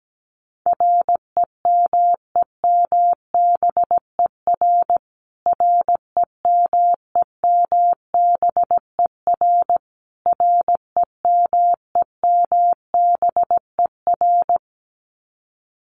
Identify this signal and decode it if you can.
Morse code, heard as a keyed tone, about 17 wpm, 700 Hz. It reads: REMEMBER REMEMBER REMEMBER
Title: Top 500 Words Repeated 17wpm